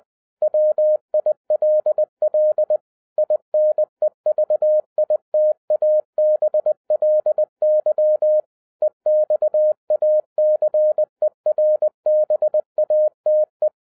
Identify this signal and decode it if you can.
Morse code, heard as a keyed tone, about 20 wpm, 610 Hz. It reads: WILL INEVITABLY EXACERBATE